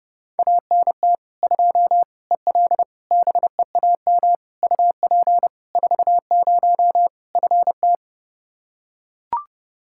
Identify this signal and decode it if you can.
Morse code, heard as a keyed tone, about 30 wpm, 700 Hz. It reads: ANT 2 EL BEAM UP 40 FT
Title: Practice QSO - Exchange 2 - Info - 30wpm